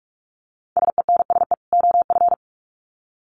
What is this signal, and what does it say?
Morse code, heard as a keyed tone, about 44 wpm, 700 Hz. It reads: SENSE OF
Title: Sets of 2 Words 45wpm